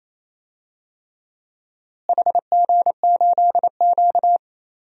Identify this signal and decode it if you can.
Morse code, heard as a keyed tone, about 28 wpm, 700 Hz. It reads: HG8Q
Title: DXCC-100 Call Signs - 4 Characters - Repeated 3x - 28wpm